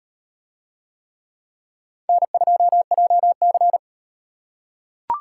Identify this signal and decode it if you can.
Morse code, heard as a keyed tone, about 36 wpm, 700 Hz. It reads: N2JC